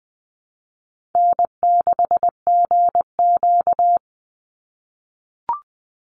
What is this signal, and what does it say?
Morse code, heard as a keyed tone, about 20 wpm, 700 Hz. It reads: N6GQ
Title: Easy Call Signs - 20wpm